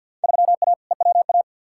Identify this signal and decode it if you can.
Morse code, heard as a keyed tone, about 47 wpm, 700 Hz. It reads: 3A EWA
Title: Field Day Exchanges Encore - 50wpm